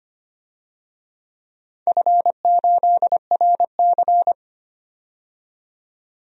Morse code, heard as a keyed tone, about 25 wpm, 700 Hz. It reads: F8RC
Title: DXCC-100 Call Signs - 4 Characters - 25wpm